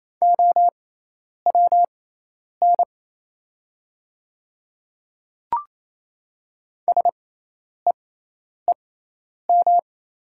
Morse code, heard as a keyed tone, about 27 wpm, spaced as for 10 wpm, 700 Hz. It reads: OWN SEEM
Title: 6x Character Spacing - Top 300 Words - 28wpm